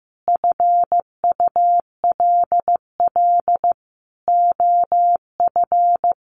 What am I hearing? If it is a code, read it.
Morse code, heard as a keyed tone, about 15 wpm, 700 Hz. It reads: FULL OF
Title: Sentences from Top 1500 Words 15wpm